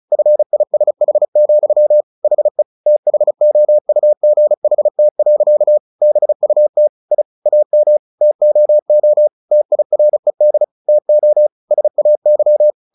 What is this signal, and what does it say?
Morse code, heard as a keyed tone, about 34 wpm, 600 Hz. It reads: FISH, HE THOUGHT. BUT I AM TOO TIRED TO SAY
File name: the old man and the sea 2 35wpm